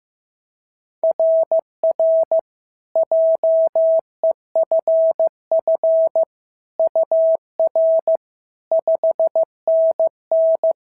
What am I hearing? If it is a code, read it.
Morse code, heard as a keyed tone, about 14 wpm, 650 Hz. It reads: RR JEFF UR 5NN
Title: SOTA QSO Practice 15wpm